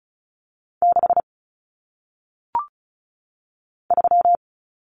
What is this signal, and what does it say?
Morse code, heard as a keyed tone, about 36 wpm, 700 Hz. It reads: 6 3
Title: Common Cut Numbers - 1-Digit 35wpm